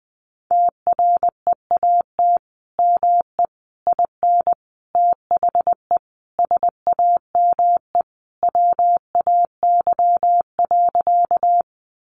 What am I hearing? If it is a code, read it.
Morse code, heard as a keyed tone, about 20 wpm, 700 Hz. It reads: TREAT ME IN THE SAME WAY.
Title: Sentences from Top 1300 Words 20wpm